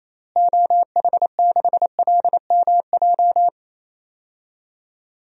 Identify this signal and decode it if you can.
Morse code, heard as a keyed tone, about 28 wpm, 700 Hz. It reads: OH6LMJ
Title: DXCC-100 Call Signs - 6 Characters - Repeated 3x - 28wpm